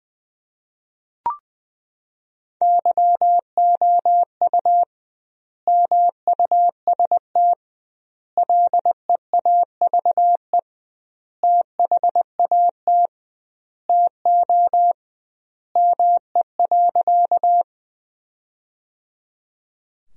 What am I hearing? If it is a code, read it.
Morse code, heard as a keyed tone, about 20 wpm, 700 Hz. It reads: YOU MUST LEAVE THAT TO ME.
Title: 2x Word-Spacing - Sentences from Top 400 Words - 20wpm